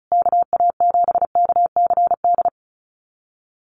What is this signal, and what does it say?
Morse code, heard as a keyed tone, about 34 wpm, 700 Hz. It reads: KA7KCD
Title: USA Call Signs Repeated 3x 35wpm